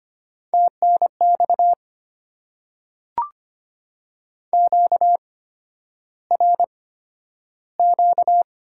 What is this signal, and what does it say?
Morse code, heard as a keyed tone, about 25 wpm, spaced as for 11 wpm, 700 Hz. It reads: TNX QRQ
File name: Top 90 QSO Elements - 8x Character Spacing - 25wpm